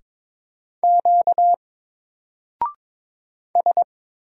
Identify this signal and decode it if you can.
Morse code, heard as a keyed tone, about 22 wpm, 700 Hz. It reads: Q S
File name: Single-Letter-Number-22wpm